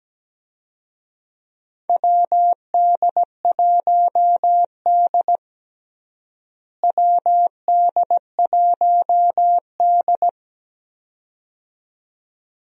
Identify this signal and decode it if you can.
Morse code, heard as a keyed tone, about 17 wpm, 700 Hz. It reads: WD1D WD1D
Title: USA Call Signs - 4 Characters - Repeated 2x - 17wpm